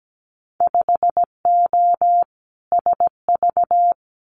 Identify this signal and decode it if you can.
Morse code, heard as a keyed tone, about 17 wpm, 700 Hz. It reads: 5O SV
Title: Winter Field Day Exchanges Encore 17wpm